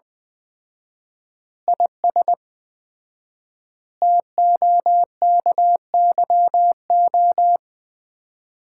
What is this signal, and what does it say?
Morse code, heard as a keyed tone, about 20 wpm, 700 Hz. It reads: IS TOKYO
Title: Common QSO Phrases - 4x Word-Spacing - 20wpm